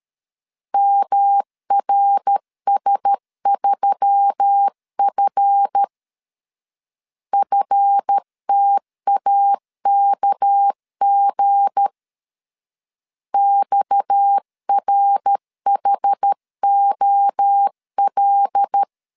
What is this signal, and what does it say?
Morse code, heard as a keyed tone, about 13 wpm, 800 Hz. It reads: MRS3F FTAKG XRHOL
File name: LEZIONI / 7 GF9Z / 7 GF 12wpm 2122